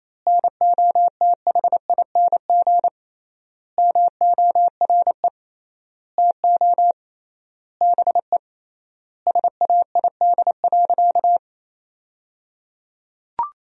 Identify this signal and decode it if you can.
Morse code, heard as a keyed tone, about 27 wpm, 700 Hz. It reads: NOTHING MORE TO BE SAID.
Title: 3x Word-Spacing - Sentences from Top 500 Words - 28wpm